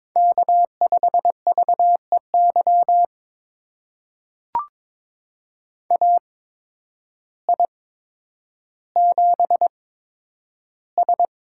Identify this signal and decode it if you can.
Morse code, heard as a keyed tone, about 22 wpm, spaced as for 10 wpm, 700 Hz. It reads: K5VEY AI7S
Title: USA Call Signs - 8x Character Spacing - 22wpm